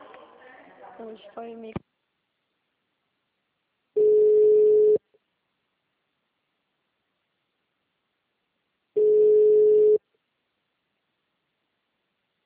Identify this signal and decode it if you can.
Ringback tone (Europe, 425 Hz), with speech.
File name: out- -2010- - - .96387